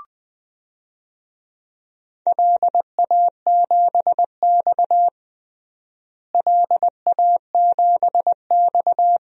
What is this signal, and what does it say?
Morse code, heard as a keyed tone, about 20 wpm, 700 Hz. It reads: LA7X LA7X
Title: DXCC-100 Call Signs - 4 Characters - Repeated 2x - 20wpm